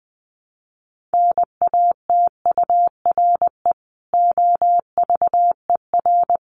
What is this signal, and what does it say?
Morse code, heard as a keyed tone, about 20 wpm, 700 Hz. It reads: NATURE OVER
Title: Sentences from Top 1500 Words 20wpm